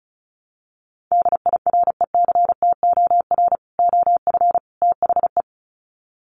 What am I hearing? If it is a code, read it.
Morse code, heard as a keyed tone, about 34 wpm, 700 Hz. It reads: DIRECTOR OF THE
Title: Sets of 3 Words 35wpm